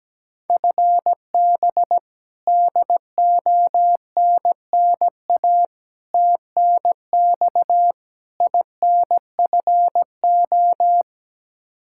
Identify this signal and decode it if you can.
Morse code, heard as a keyed tone, about 17 wpm, 700 Hz. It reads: FB DONNA TNX INFO
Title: Practice QSO - Exchange 3 - More Info - 17wpm